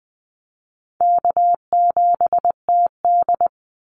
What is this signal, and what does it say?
Morse code, heard as a keyed tone, about 20 wpm, 700 Hz. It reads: K7TD
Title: Call Signs - 20wpm